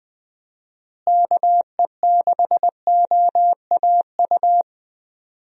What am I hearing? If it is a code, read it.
Morse code, heard as a keyed tone, about 20 wpm, 700 Hz. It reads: KE6OAU
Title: USA Call Signs - 6 Characters - Repeated 3x - 20wpm